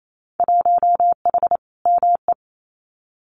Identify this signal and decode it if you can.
Morse code, heard as a keyed tone, about 28 wpm, 700 Hz. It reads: 1H ME
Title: Winter Field Day Exchanges 28wpm